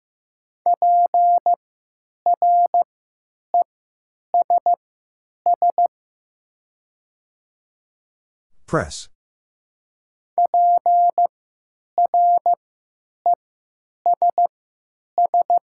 Morse code, heard as a keyed tone, about 15 wpm, spaced as for 9 wpm, 700 Hz. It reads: PRESS PRESS
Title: 3x Character Spacing - Top 300 Words - 15wpm